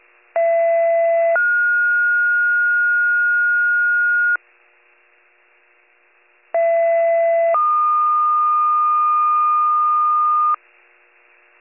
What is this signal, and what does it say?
Two-tone page, heard as a one-second tone, then a three-second tone, about 670 then 1430 Hz, about 670 then 1220 Hz. It is heard from 0.4 to 10.5 s.